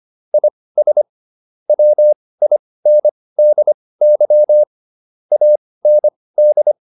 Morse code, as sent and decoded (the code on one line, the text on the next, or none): .. ... / .-- .. -. -.. -.-- / .- -. -..
IS WINDY AND